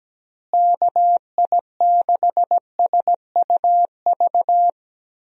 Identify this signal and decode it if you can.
Morse code, heard as a keyed tone, about 17 wpm, 700 Hz. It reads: KI6SUV
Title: USA Call Signs - 6 Characters - Repeated 2x - 17wpm